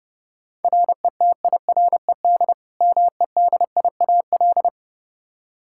Morse code, heard as a keyed tone, about 30 wpm, 700 Hz. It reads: RETIRED MEDIAL
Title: Practice QSO - Exchange 3 - More Info - 30wpm